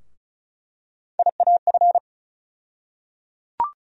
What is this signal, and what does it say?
Morse code, heard as a keyed tone, about 34 wpm, 700 Hz. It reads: IAF
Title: USA Call Sign Postfixes - 3 Letters 35wpm